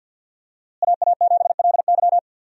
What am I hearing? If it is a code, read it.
Morse code, heard as a keyed tone, about 52 wpm, 700 Hz. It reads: AA7LY
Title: USA Call Signs - 5 Characters - Repeated 3x - 50wpm